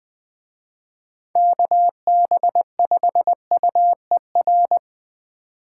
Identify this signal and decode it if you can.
Morse code, heard as a keyed tone, about 21 wpm, 700 Hz. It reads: KB5UER